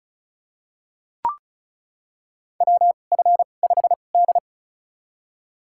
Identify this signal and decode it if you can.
Morse code, heard as a keyed tone, about 34 wpm, spaced as for 25 wpm, 700 Hz. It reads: WF5D
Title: USA Call Signs - 2x Character Spacing - 35wpm